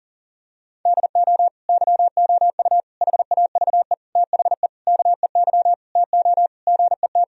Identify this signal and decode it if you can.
Morse code, heard as a keyed tone, about 40 wpm, 690 Hz. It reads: DO YOU HAVE THE KEY TO GET
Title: Sentences from Top 1000 Words - Review Part 4 - 40wpm